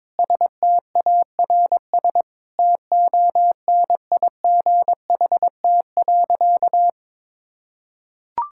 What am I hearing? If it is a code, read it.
Morse code, heard as a keyed tone, about 22 wpm, 700 Hz. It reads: STARS TONIGHT.
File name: practice / 22wpm / Sentences with 3-Word Phrases - 22wpm